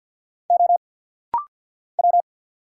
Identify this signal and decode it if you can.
Morse code, heard as a keyed tone, about 50 wpm, 690 Hz. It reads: O W
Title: Single Letter-Number Alt - Rapid-Fire 50wpm